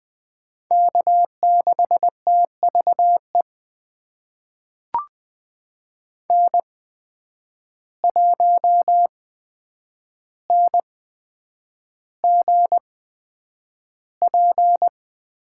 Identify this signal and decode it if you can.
Morse code, heard as a keyed tone, about 20 wpm, spaced as for 9 wpm, 700 Hz. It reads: K6TVE N1NGP
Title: USA Call Signs - 5 Characters - 8x Character Spacing - 20wpm